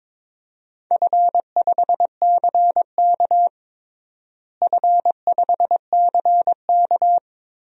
Morse code, heard as a keyed tone, about 22 wpm, 700 Hz. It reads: F5CK F5CK
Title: DXCC-100 Call Signs - 4 Characters - Repeated 3x - 22wpm